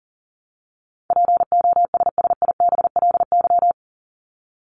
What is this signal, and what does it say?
Morse code, heard as a keyed tone, about 40 wpm, 700 Hz. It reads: POSSIBLY
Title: LY Suffix - Top 100 Words with Base Words 40wpm